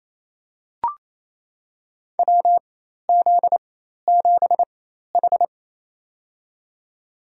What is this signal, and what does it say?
Morse code, heard as a keyed tone, about 28 wpm, spaced as for 13 wpm, 700 Hz. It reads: WZ7H
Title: USA Call Signs - 4 Characters - 4x Character Spacing - 28wpm